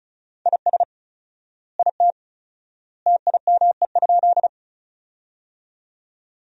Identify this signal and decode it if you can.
Morse code, heard as a keyed tone, about 34 wpm, 700 Hz. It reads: IS IT TIME?
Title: 4x Word-Spacing - Sentences from Top 100 Words - 35wpm